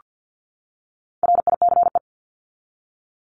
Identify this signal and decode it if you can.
Morse code, heard as a keyed tone, about 48 wpm, 700 Hz. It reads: RICE
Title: Top 1501-1750 Words 50wpm